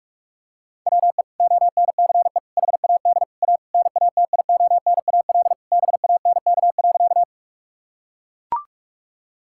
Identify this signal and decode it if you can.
Morse code, heard as a keyed tone, about 41 wpm, 700 Hz. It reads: WE ONCE HAD A NATIONAL BANK.
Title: Sentences from Top 1000 Words - Review Part 6 - 45wpm